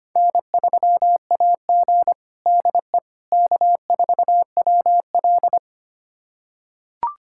Morse code, heard as a keyed tone, about 25 wpm, 700 Hz. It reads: N3AG DE K4WL
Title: USA Callsign DE Callsign - 4 Characters - 25wpm